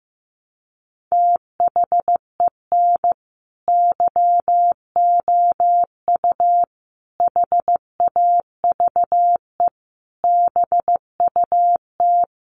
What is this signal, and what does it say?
Morse code, heard as a keyed tone, about 15 wpm, 700 Hz. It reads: THEN YOU HAVE BUT
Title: Sentences from Top 1000 Words 15wpm